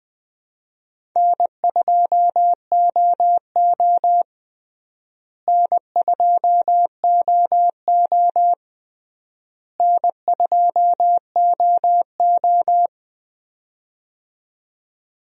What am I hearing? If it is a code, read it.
Morse code, heard as a keyed tone, about 20 wpm, 700 Hz. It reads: N2OO N2OO N2OO
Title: USA Call Signs - 4 Characters - Repeated 3x - 20wpm